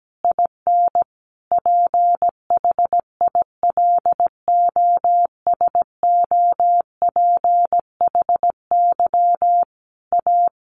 Morse code, heard as a keyed tone, about 17 wpm, 700 Hz. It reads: IN PHILOSOPHY A
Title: Sentences from Top 1400 Words 17wpm